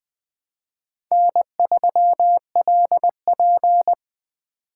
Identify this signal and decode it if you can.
Morse code, heard as a keyed tone, about 20 wpm, 700 Hz. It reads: N3LP
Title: Speed Racing - USA Call Signs 17wpm